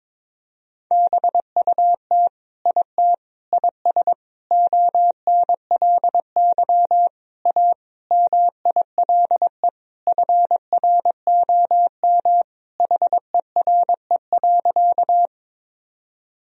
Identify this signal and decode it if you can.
Morse code, heard as a keyed tone, about 22 wpm, 700 Hz. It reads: BUT IT IS ONLY A MILE FROM HERE.
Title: Sentences from Top 1000 Words - Review Part 1 - 22wpm